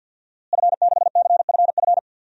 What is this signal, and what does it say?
Morse code, heard as a keyed tone, about 52 wpm, 700 Hz. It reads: F6CFF
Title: DXCC-100 Call Signs - 5 Characters - Repeated 3x - 50wpm